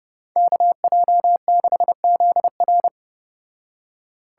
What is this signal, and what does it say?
Morse code, heard as a keyed tone, about 29 wpm, 700 Hz. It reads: KJ6ZR